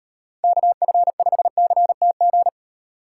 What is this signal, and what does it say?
Morse code, heard as a keyed tone, about 33 wpm, 700 Hz. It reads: KF5CTG